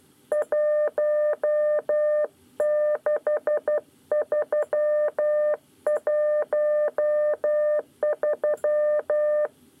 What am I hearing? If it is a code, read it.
Morse code, heard as a keyed tone, about 12 wpm, 600 Hz. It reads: 16313
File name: 9 lezione cw 2